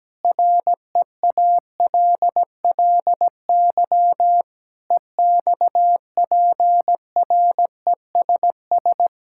Morse code, heard as a keyed tone, about 17 wpm, 700 Hz. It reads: REALLY EXPRESS
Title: Sentences from Top 1600 Words 17wpm